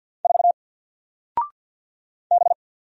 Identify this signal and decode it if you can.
Morse code, heard as a keyed tone, about 49 wpm, 700 Hz. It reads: 4 B
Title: Single Letter-Number 50wpm